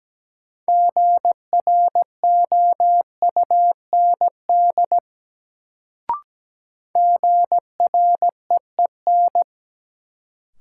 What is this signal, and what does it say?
Morse code, heard as a keyed tone, about 17 wpm, 700 Hz. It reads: GROUND GREEN
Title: Sentences from Top 500 Words - New Word Drill - 17wpm